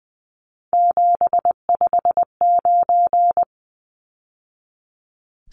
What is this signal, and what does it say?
Morse code, heard as a keyed tone, about 20 wpm, 700 Hz. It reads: 759
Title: Numbers - 3-Digit 20wpm